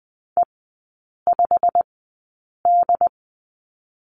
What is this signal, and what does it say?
Morse code, heard as a keyed tone, about 20 wpm, spaced as for 8 wpm, 700 Hz. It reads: E5D
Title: 127 Lesson 21e - Number 6 - Three Characters 20wpm